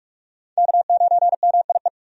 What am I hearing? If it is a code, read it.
Morse code, heard as a keyed tone, about 43 wpm, 700 Hz. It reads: K9MIE
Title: USA Call Signs - 5 Characters - 45wpm